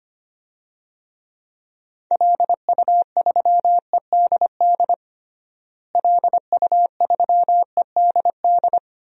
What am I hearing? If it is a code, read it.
Morse code, heard as a keyed tone, about 25 wpm, 700 Hz. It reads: LU3EDD LU3EDD
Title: DXCC-100 Call Signs - 6 Characters - Repeated 2x - 25wpm